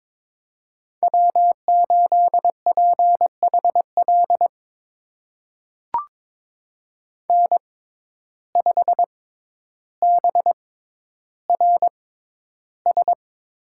Morse code, heard as a keyed tone, about 22 wpm, spaced as for 11 wpm, 700 Hz. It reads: W8PHL N5BRS
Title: USA Call Signs - 5 Characters - 6x Character Spacing - 22wpm